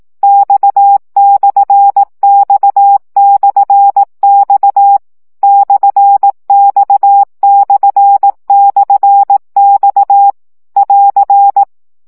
Morse code, heard as a keyed tone, about 18 wpm, 800 Hz. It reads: X/X/X /X//X <AR>
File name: CW gyakorlas x per 15 baud